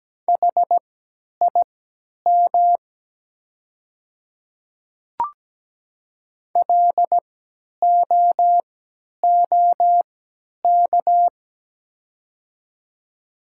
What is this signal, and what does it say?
Morse code, heard as a keyed tone, about 17 wpm, spaced as for 10 wpm, 690 Hz. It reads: HIM LOOK
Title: 3x Character Spacing - Top 100 Words - 17wpm